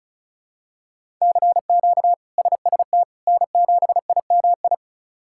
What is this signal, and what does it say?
Morse code, heard as a keyed tone, about 36 wpm, 700 Hz. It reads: CQ SST N7IMI
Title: K1USN SST QSO Practice - 35wpm